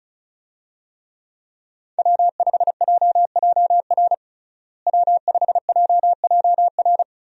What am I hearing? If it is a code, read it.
Morse code, heard as a keyed tone, about 34 wpm, 700 Hz. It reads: W5JJR W5JJR
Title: USA Call Signs - 5 Characters - Repeated 3x - 35wpm